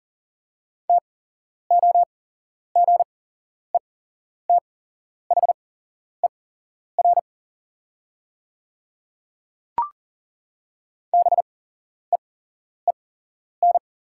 Morse code, heard as a keyed tone, about 38 wpm, spaced as for 11 wpm, 700 Hz. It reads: TOGETHER BEEN
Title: 8x Character Spacing - Top 300 Words - 40wpm